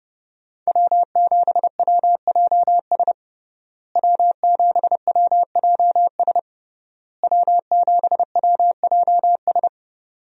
Morse code, heard as a keyed tone, about 30 wpm, 700 Hz. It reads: W7WJS W7WJS W7WJS
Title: USA Call Signs - 5 Characters - Repeated 3x - 30wpm